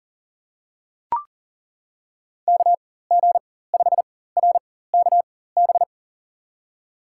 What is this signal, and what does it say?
Morse code, heard as a keyed tone, about 40 wpm, spaced as for 19 wpm, 700 Hz. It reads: KG5RKB
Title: USA Call Signs - 6 Characters - 4x Character Spacing - 40wpm